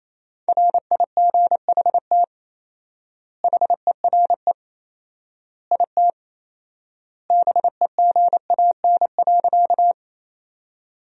Morse code, heard as a keyed tone, about 28 wpm, 700 Hz. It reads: RIGHT HERE IT BEGAN.
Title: 4x Word-Spacing - Sentences from Top 300 Words - 28wpm